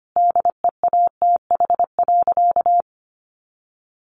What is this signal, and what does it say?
Morse code, heard as a keyed tone, about 25 wpm, 700 Hz. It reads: DEATH.
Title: Sentences from Top 1100 Words 25wpm